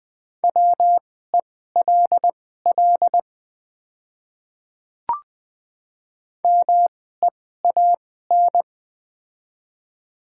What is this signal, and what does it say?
Morse code, heard as a keyed tone, about 20 wpm, spaced as for 14 wpm, 700 Hz. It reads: WELL MEAN
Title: 2x Character Spacing - Top 500 Words - 20wpm